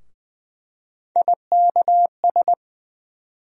Morse code, heard as a keyed tone, about 20 wpm, 700 Hz. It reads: IKS
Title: USA Call Sign Postfixes - 3 Letters 20wpm